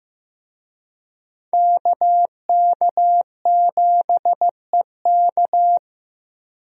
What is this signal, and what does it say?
Morse code, heard as a keyed tone, about 16 wpm, 700 Hz. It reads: KK7EK
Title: USA Call Signs - 5 Characters - Repeated 2x - 15wpm